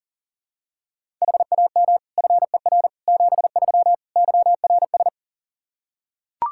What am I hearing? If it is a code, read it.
Morse code, heard as a keyed tone, about 40 wpm, 700 Hz. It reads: HAM FER 73 YRS
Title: Practice QSO - Exchange 3 - More Info - 40wpm